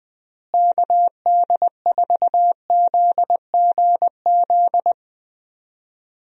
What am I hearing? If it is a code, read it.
Morse code, heard as a keyed tone, about 21 wpm, 700 Hz. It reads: KD4ZGZ
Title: USA Call Signs - 6 Characters - Repeated 3x - 20wpm